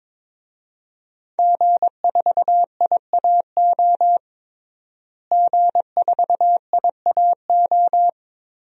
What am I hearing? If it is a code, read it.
Morse code, heard as a keyed tone, about 22 wpm, 700 Hz. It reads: G4IAO G4IAO
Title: DXCC-100 Call Signs - 5 Characters - Repeated 2x - 22wpm